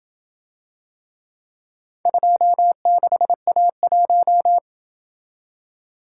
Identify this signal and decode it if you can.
Morse code, heard as a keyed tone, about 26 wpm, 700 Hz. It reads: 26A1